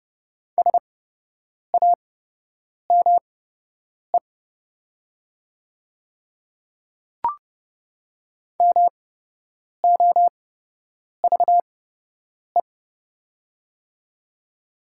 Morse code, heard as a keyed tone, about 30 wpm, spaced as for 8 wpm, 700 Hz. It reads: SAME MOVE
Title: 8x Character Spacing - Top 200 Words - 30wpm